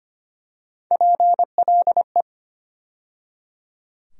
Morse code, heard as a keyed tone, about 25 wpm, 700 Hz. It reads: PLE
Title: USA Call Sign Postfixes - 3 Letters 25wpm